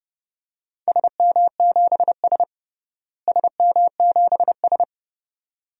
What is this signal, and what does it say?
Morse code, heard as a keyed tone, about 31 wpm, 700 Hz. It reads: SM7S SM7S
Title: DXCC-100 Call Signs - 4 Characters - Repeated 2x - 30wpm